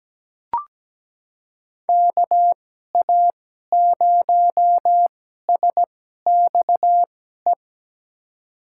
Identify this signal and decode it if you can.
Morse code, heard as a keyed tone, about 17 wpm, spaced as for 12 wpm, 700 Hz. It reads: KA0SXE
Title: USA Call Signs - 6 Characters - 2x Character Spacing - 17wpm